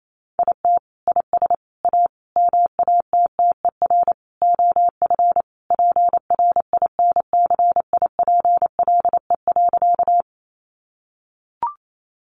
Morse code, heard as a keyed tone, about 28 wpm, 700 Hz. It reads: IT IS A MATTER OF PRINCIPLE.
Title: Sentences from Top 1400 Words 28wpm